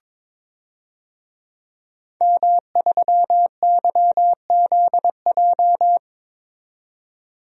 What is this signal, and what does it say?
Morse code, heard as a keyed tone, about 22 wpm, 700 Hz. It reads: M3YZJ
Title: DXCC-100 Call Signs - 5 Characters - 22wpm